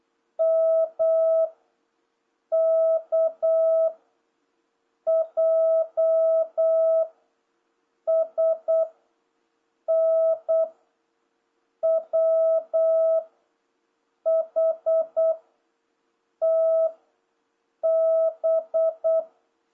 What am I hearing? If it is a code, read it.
Morse code, heard as a keyed tone, about 8 wpm, 650 Hz. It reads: M K J S N W H T B